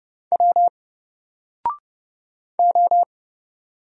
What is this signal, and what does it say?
Morse code, heard as a keyed tone, about 30 wpm, 700 Hz. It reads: W O
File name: Single-Letter-Number-30wpm